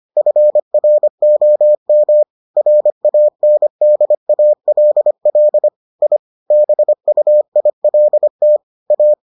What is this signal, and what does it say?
Morse code, heard as a keyed tone, about 25 wpm, 600 Hz. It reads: FROM RANDALL I BUILT A